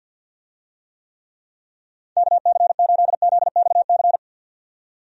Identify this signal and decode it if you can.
Morse code, heard as a keyed tone, about 50 wpm, 700 Hz. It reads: KC8ZXC
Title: USA Call Signs - 6 Characters - 50wpm